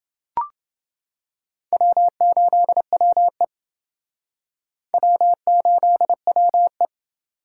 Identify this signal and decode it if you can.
Morse code, heard as a keyed tone, about 30 wpm, 700 Hz. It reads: W8WE W8WE